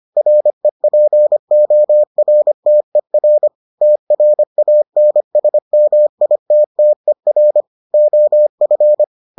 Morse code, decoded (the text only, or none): REPORTER TRANSMITTER OF